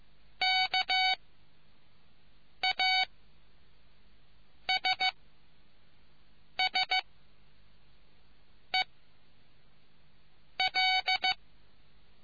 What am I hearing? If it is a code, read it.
Morse code, heard as a keyed tone, about 15 wpm, spaced as for 5 wpm, 750 Hz. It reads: KASSEL